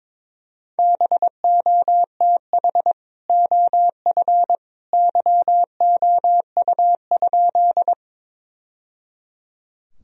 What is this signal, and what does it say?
Morse code, heard as a keyed tone, about 22 wpm, 700 Hz. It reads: BOTH OF YOU?
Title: Sentences from Top 1000 Words - Review Part 1 - 22wpm